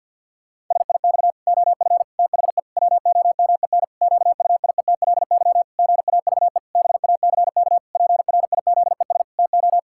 Morse code, heard as a keyed tone, about 52 wpm, 700 Hz. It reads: SIX OF THE WOMEN QUIETLY GAVE BACK PRIZES TO
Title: Pangram Challenge v2 - 50wpm